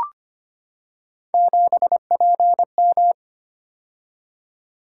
Morse code, heard as a keyed tone, about 25 wpm, 700 Hz. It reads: 7PM
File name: Three Letter-Number Combinations - 25wpm